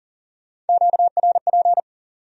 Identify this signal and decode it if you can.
Morse code, heard as a keyed tone, about 52 wpm, 700 Hz. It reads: QRP